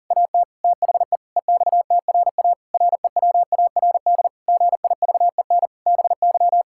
Morse code, heard as a keyed tone, about 40 wpm, 700 Hz. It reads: AT THE EXTRA REWARD GIVEN BY
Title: Pangram Challenge v2 - 40wpm